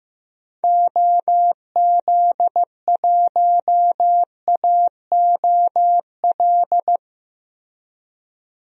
Morse code, heard as a keyed tone, about 15 wpm, 700 Hz. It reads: OZ1AOL
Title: DXCC-100 Call Signs - 6 Characters - Repeated 2x - 15wpm